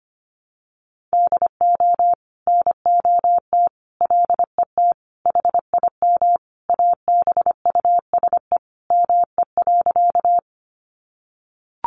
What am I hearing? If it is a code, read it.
Morse code, heard as a keyed tone, about 25 wpm, 700 Hz. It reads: DO NOT LET HIM ABUSE ME.
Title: Sentences from Top 1300 Words 25wpm